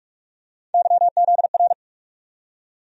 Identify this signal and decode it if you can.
Morse code, heard as a keyed tone, about 45 wpm, 700 Hz. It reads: YZR
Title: USA Call Sign Postfixes - 3 Letters 45wpm